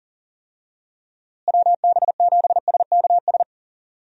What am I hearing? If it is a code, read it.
Morse code, heard as a keyed tone, about 40 wpm, 700 Hz. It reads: WB7SKS
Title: USA Call Signs - 6 Characters - Repeated 3x - 40wpm